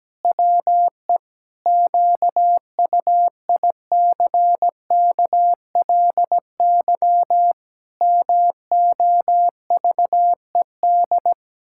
Morse code, heard as a keyed tone, about 17 wpm, 700 Hz. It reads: WE QUICKLY MOVED
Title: Pangram Challenge v2 - 17wpm